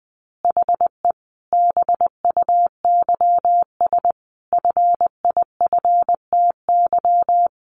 Morse code, heard as a keyed tone, about 20 wpm, 700 Hz. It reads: HE BUYS FIFTY
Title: Additional Easy Phrases 20wpm